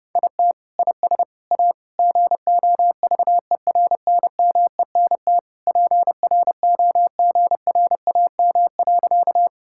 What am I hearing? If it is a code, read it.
Morse code, heard as a keyed tone, about 30 wpm, 700 Hz. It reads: IT IS A GOVERNMENT PROGRAM.
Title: Sentences from Top 1000 Words - Review Part 5 - 30wpm